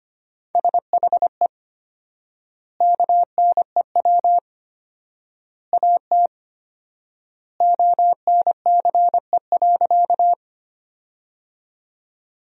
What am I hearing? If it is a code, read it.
Morse code, heard as a keyed tone, about 25 wpm, 700 Hz. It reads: SHE KNEW AT ONCE.
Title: 4x Word-Spacing - Sentences from Top 400 Words - 25wpm